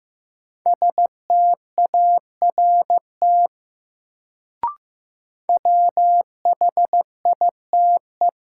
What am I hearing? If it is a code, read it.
Morse code, heard as a keyed tone, about 15 wpm, 700 Hz. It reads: START WHITE
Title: Sentences from Top 300 Words - New Word Drill - 15wpm